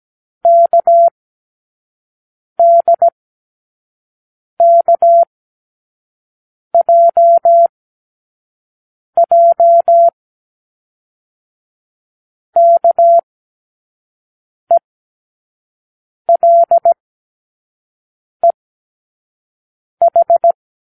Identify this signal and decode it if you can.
Morse code, heard as a keyed tone, about 17 wpm, spaced as for 5 wpm, 680 Hz. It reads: KDKJJ KELEH
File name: k7qo / Lesson18(A-L)